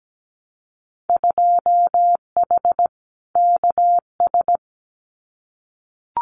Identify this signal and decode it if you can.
Morse code, heard as a keyed tone, about 17 wpm, 700 Hz. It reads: 2H KS
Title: Winter Field Day Exchanges Encore 17wpm